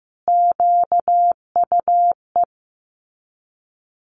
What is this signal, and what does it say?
Morse code, heard as a keyed tone, about 15 wpm, 700 Hz. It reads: QUE
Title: Multi-letter Phonemes Repeated 3x 15wpm